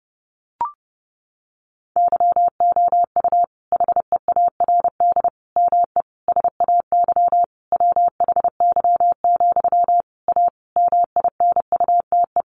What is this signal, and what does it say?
Morse code, heard as a keyed tone, about 30 wpm, 700 Hz. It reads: YOU HEARD ME SAY WHY, A MINUTE
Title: Sentences from Top 500 Words 30wpm